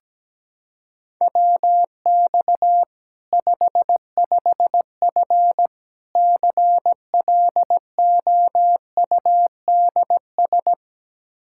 Morse code, heard as a keyed tone, about 17 wpm, 700 Hz. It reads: WX 55F CLOUDS
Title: Practice QSO - Exchange 2 - Info - 17wpm